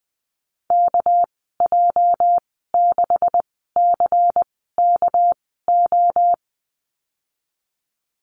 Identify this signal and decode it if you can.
Morse code, heard as a keyed tone, about 20 wpm, spaced as for 15 wpm, 700 Hz. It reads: KJ6CKO
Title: 238 Lesson 38g - Letter q - Call Signs 20wpm